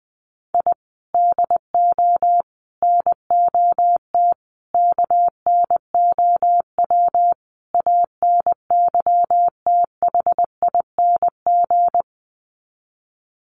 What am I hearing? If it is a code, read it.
Morse code, heard as a keyed tone, about 20 wpm, 700 Hz. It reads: I DO NOT KNOW ANYTHING
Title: Sets of 5 Words Encore 20wpm